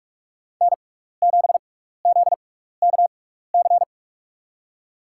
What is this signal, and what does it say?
Morse code, heard as a keyed tone, about 44 wpm, spaced as for 15 wpm, 700 Hz. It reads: N7ZKC
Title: USA Call Signs - 5 Characters - 6x Character Spacing - 45wpm